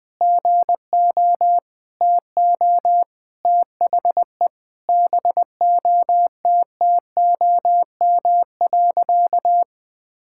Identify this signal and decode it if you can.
Morse code, heard as a keyed tone, about 21 wpm, 700 Hz. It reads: GO TO THE BOTTOM.
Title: Sentences from Top 1000 Words - Review Part 6 - 20wpm